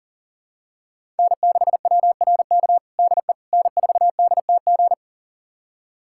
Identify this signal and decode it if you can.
Morse code, heard as a keyed tone, about 37 wpm, 700 Hz. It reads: N6WRK DE N4DTG
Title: USA Callsign DE Callsign 40wpm